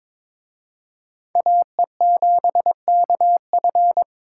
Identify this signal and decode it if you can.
Morse code, heard as a keyed tone, about 26 wpm, 700 Hz. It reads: AE7KF
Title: Speed Racing - USA 2x2 Call Signs 22wpm